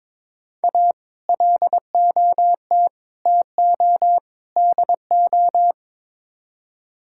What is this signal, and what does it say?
Morse code, heard as a keyed tone, about 22 wpm, 700 Hz. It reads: A LOT TO DO
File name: practice / 22wpm / Build A Phrase - 5-Word Phrases - 22wpm